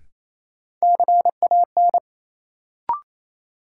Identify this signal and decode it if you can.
Morse code, heard as a keyed tone, about 28 wpm, 700 Hz. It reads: CAN
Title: Sets of 2 Words - New Word Drill - 28wpm